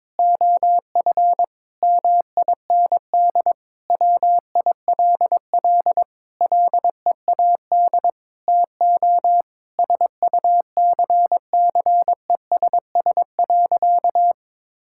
Morse code, heard as a keyed tone, about 22 wpm, 700 Hz. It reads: OF MIND WILL LEAD TO SUCCESS.
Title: Sentences from Top 1000 Words - Review Part 6 - 22wpm